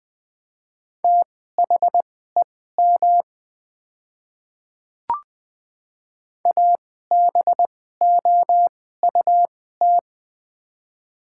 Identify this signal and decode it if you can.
Morse code, heard as a keyed tone, about 21 wpm, spaced as for 15 wpm, 700 Hz. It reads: THEM ABOUT
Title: 2x Character Spacing - Top 200 Words - 20wpm